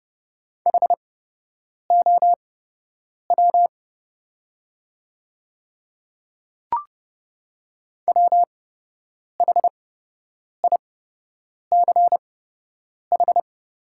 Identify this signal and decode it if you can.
Morse code, heard as a keyed tone, about 30 wpm, spaced as for 8 wpm, 690 Hz. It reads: HOW WHICH
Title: 8x Character Spacing - Top 200 Words - 30wpm